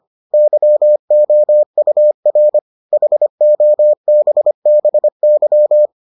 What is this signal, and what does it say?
Morse code, heard as a keyed tone, about 25 wpm, 600 Hz. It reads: YOUR HOBBY